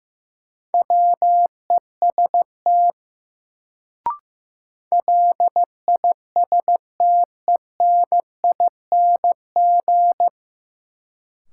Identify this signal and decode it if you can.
Morse code, heard as a keyed tone, about 15 wpm, 700 Hz. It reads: WEST LISTENING
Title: Sets of 5 Words - New Word Drill - 15wpm